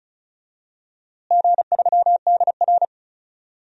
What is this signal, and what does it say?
Morse code, heard as a keyed tone, about 36 wpm, 700 Hz. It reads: G3DR
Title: DXCC-100 Call Signs - 4 Characters - Repeated 3x - 35wpm